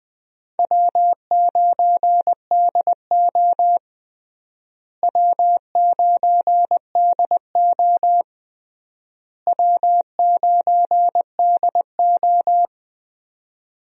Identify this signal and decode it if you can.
Morse code, heard as a keyed tone, about 20 wpm, 700 Hz. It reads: W9DO W9DO W9DO
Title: USA Call Signs - 4 Characters - Repeated 3x - 20wpm